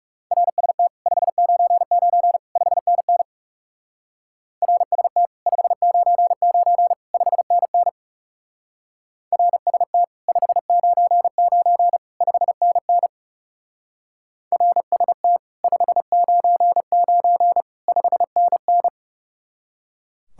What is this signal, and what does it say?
Morse code, heard as a keyed tone, about 38 wpm, 700 Hz. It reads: RST 599 5NN RST 599 5NN RST 599 5NN RST 599 5NN